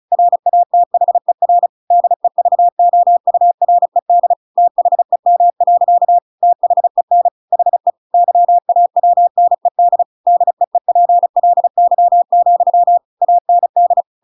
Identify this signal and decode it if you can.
Morse code, heard as a keyed tone, about 34 wpm, 700 Hz. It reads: RATHER DEVOURED THEM. THEN HE YAWNED DEEPLY, AND
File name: Pinocchio / Pinocchio ch7 35 35 ik1hge